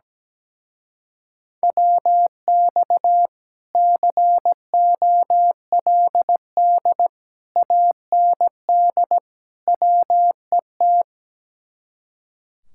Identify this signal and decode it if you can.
Morse code, heard as a keyed tone, about 17 wpm, 700 Hz. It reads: WX COLD AND WET
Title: Speed-Racing Common QSO Phrases - 17wpm